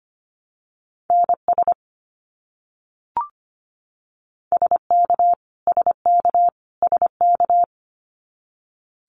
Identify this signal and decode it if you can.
Morse code, heard as a keyed tone, about 25 wpm, 700 Hz. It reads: NS SK SK SK
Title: Canadian Provinces and Territories - Repeated 3x - 25wpm